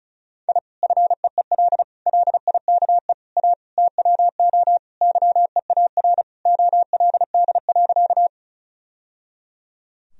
Morse code, heard as a keyed tone, about 36 wpm, 700 Hz. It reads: I FEEL LIKE A TWO YEAR OLD.
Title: Sentences from Top 1000 Words - Review Part 1 - 35wpm